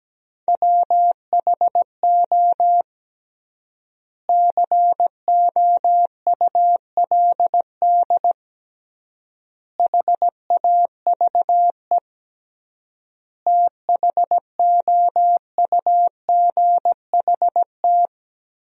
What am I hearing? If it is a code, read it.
Morse code, heard as a keyed tone, about 17 wpm, 700 Hz. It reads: WHO COULD HAVE THOUGHT
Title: 3x Word-Spacing - Sentences from Top 300 Words - 17wpm